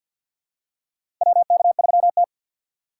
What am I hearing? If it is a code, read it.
Morse code, heard as a keyed tone, about 53 wpm, 700 Hz. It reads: WK3T
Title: USA Call Signs - 4 Characters - 6x Character Spacing - 50wpm